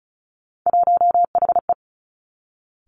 Morse code, heard as a keyed tone, about 36 wpm, 700 Hz. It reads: 1HE
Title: Three Challenging Letter-Number Combinations 35wpm